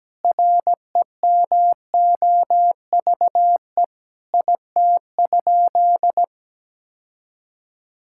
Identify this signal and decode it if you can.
Morse code, heard as a keyed tone, about 17 wpm, 700 Hz. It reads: REMOVE IT?
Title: Sentences from Top 1000 Words - Review Part 5 - 17wpm